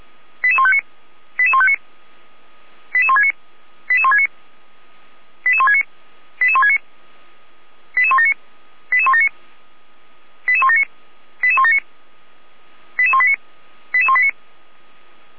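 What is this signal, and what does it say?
Five-tone selective call (ZVEI), heard as quick runs of five tones, tones of about 70 ms. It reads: 88159 88159 88169 88169 88169 88169 88179 88179 88179 88179 88189 88189
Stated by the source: ZVEI five-tone selective call received on SDR.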